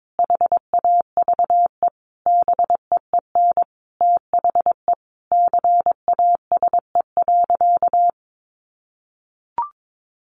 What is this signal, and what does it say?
Morse code, heard as a keyed tone, about 22 wpm, 700 Hz. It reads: HAVE BEEN THE CASE.
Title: Sentences from Top 1300 Words 22wpm